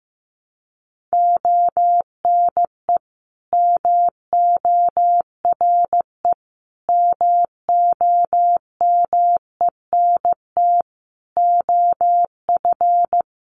Morse code, heard as a keyed tone, about 15 wpm, 700 Hz. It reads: ONE MORE MOMENT OF
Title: Sentences from Top 1300 Words 15wpm